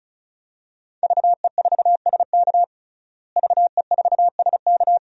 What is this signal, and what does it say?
Morse code, heard as a keyed tone, about 35 wpm, 700 Hz. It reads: VE4SK VE4SK
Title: DXCC-100 Call Signs - 5 Characters - Repeated 2x - 35wpm